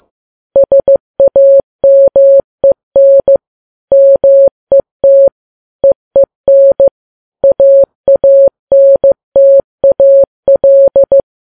Morse code, heard as a keyed tone, about 15 wpm, 550 Hz. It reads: SAMEN MET EEN AANTAL
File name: Files Web tls / Morse on9fv / morse15kps